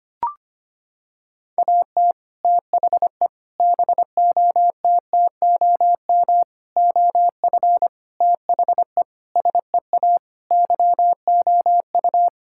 Morse code, heard as a keyed tone, about 25 wpm, 700 Hz. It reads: AT THE BOTTOM OF THE SEA YOU
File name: practice / 25wpm / Sentences with 3-Word Phrases - 25wpm